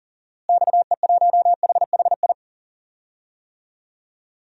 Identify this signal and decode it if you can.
Morse code, heard as a keyed tone, about 40 wpm, 700 Hz. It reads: XE1HHI
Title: DXCC-100 Call Signs - 6 Characters - Repeated 3x - 40wpm